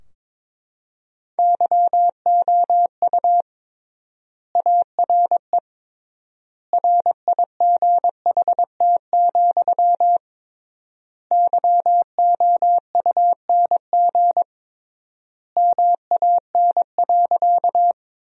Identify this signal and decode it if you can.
Morse code, heard as a keyed tone, about 21 wpm, 700 Hz. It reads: YOU ARE RIGHT, YOUNG MAN.
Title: 3x Word-Spacing - Sentences from Top 400 Words - 22wpm